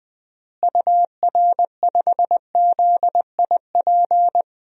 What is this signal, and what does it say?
Morse code, heard as a keyed tone, about 20 wpm, 700 Hz. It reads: UR5ZIP
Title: DXCC-100 Call Signs - 6 Characters - Repeated 3x - 20wpm